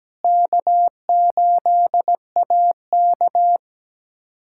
Morse code, heard as a keyed tone, about 17 wpm, 700 Hz. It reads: K8AK
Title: USA Callsign DE Callsign - 4 Characters - 17wpm